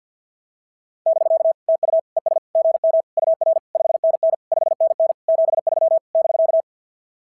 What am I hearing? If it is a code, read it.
Morse code, heard as a keyed tone, about 50 wpm, 660 Hz. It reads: <BK> TU ES GM UR 5NN 5NN 73 <BK>
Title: SOTA QSO Practice 50wpm